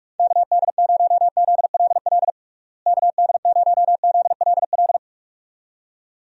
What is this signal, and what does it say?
Morse code, heard as a keyed tone, about 41 wpm, 700 Hz. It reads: KD0ZLL KD0ZLL
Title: USA Call Signs - 6 Characters - Repeated 2x - 45wpm